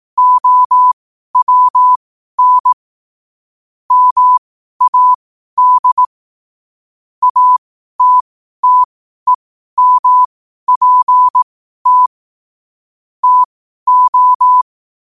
Morse code, heard as a keyed tone, about 18 wpm, spaced as for 12 wpm, 1000 Hz. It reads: OWN MAD ATTEMPT TO
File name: qotdfiles / 12wpm